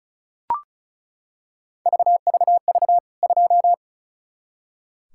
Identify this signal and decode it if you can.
Morse code, heard as a keyed tone, about 36 wpm, 700 Hz. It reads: VVV 2
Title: K1USN SST QSO Practice 35wpm